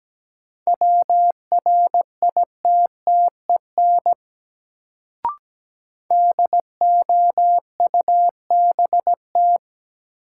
Morse code, heard as a keyed tone, about 17 wpm, 700 Hz. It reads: WRITTEN DOUBT
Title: Sentences from Top 1500 Words - New Word Drill - 17wpm